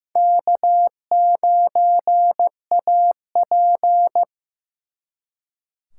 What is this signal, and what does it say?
Morse code, heard as a keyed tone, about 18 wpm, 700 Hz. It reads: K9AP
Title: Speed Racing - USA Call Signs 15wpm